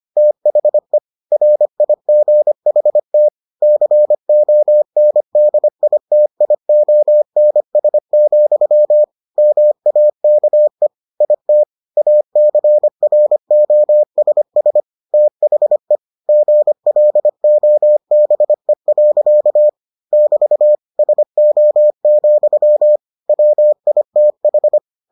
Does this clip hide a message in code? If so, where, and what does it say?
Yes: Morse code from 0.2 to 24.8 s, heard as a keyed tone, about 25 wpm, 600 Hz. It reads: THE RIGHT CONDITIONS, MAKE IT ACROSS THE GLOBE. <BT> SO, WITH